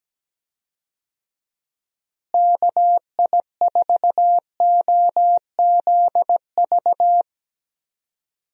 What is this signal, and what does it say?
Morse code, heard as a keyed tone, about 17 wpm, 700 Hz. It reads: KI4OZV
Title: USA Call Signs - 6 Characters - Repeated 3x - 17wpm